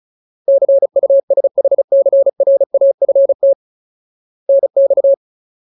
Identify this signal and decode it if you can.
Morse code, heard as a keyed tone, about 34 wpm, 550 Hz. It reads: CUSHCRAFT NX
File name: Ultimate Word List / 35WPM Ultimate